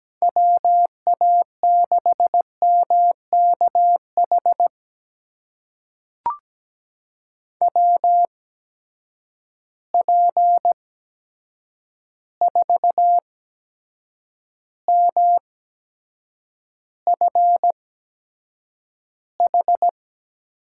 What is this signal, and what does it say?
Morse code, heard as a keyed tone, about 17 wpm, spaced as for 7 wpm, 700 Hz. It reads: WA6MKH WP4MFH
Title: USA Call Signs - 8x Character Spacing - 17wpm